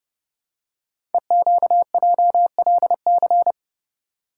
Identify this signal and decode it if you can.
Morse code, heard as a keyed tone, about 28 wpm, 700 Hz. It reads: EQJLC
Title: Code Groups - 5 Letter:Numbers - 30wpm